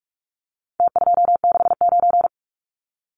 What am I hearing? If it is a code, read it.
Morse code, heard as a keyed tone, about 44 wpm, 700 Hz. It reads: T269
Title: Numbers - Contest Style - 4-Digit 45wpm